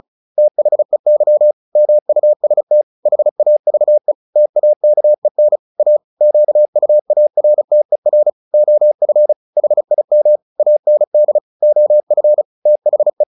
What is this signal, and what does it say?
Morse code, heard as a keyed tone, about 34 wpm, 600 Hz. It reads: THEY MUST HAVE TAKEN A QUARTER OF HIM AND OF THE
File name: the old man and the sea 2 35wpm